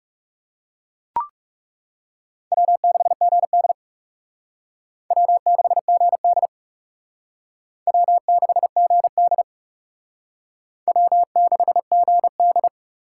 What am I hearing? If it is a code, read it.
Morse code, heard as a keyed tone, about 38 wpm, 700 Hz. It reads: W6GD W6GD W6GD W6GD